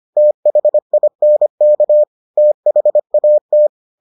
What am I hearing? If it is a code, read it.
Morse code, heard as a keyed tone, about 25 wpm, 600 Hz. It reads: THINK THAT